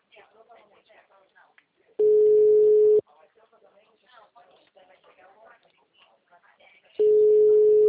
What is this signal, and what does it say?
Ringback tone (Europe, 425 Hz), with speech.